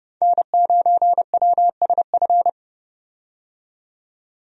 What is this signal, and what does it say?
Morse code, heard as a keyed tone, about 29 wpm, 700 Hz. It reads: N9WSF
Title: USA Call Signs - 5 Characters - 30wpm